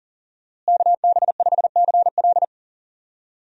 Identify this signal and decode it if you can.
Morse code, heard as a keyed tone, about 39 wpm, 700 Hz. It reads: KB5CL
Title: USA Call Signs - 5 Characters - Repeated 3x - 40wpm